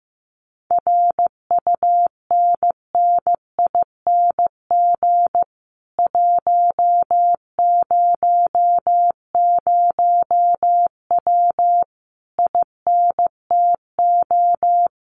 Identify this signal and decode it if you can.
Morse code, heard as a keyed tone, about 15 wpm, 700 Hz. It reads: RUNNING 100W INTO
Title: Common QSO Phrases Encore 15wpm